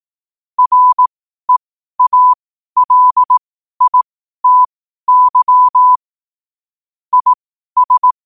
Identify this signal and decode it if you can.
Morse code, heard as a keyed tone, about 18 wpm, spaced as for 12 wpm, 1000 Hz. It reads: REALITY IS
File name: qotdfiles / 12wpm